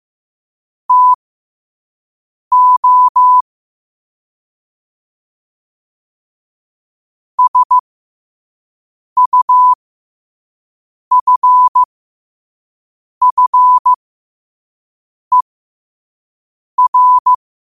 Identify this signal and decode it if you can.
Morse code, heard as a keyed tone, about 15 wpm, spaced as for 5 wpm, 1000 Hz. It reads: TO SUFFER
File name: qotdfiles / 5wpm